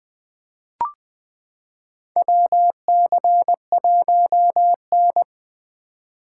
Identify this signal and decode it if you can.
Morse code, heard as a keyed tone, about 20 wpm, 700 Hz. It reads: WC1N
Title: 2023 POTA Activator Call Signs - 20wpm